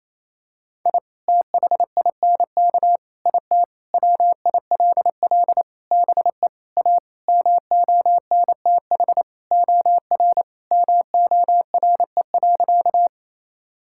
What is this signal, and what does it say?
Morse code, heard as a keyed tone, about 28 wpm, 700 Hz. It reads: I THINK IT WILL BE A MONTH OR MORE.
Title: Sentences from Top 1000 Words - Review Part 5 - 28wpm